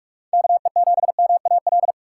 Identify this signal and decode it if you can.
Morse code, heard as a keyed tone, about 60 wpm, spaced as for 53 wpm, 700 Hz. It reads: KE7MAL
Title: Speed Racing - USA 2x3 Call Signs 45wpm